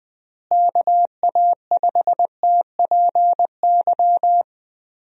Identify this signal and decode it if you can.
Morse code, heard as a keyed tone, about 20 wpm, 700 Hz. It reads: KA5TPY
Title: USA Call Signs - 6 Characters - Repeated 3x - 20wpm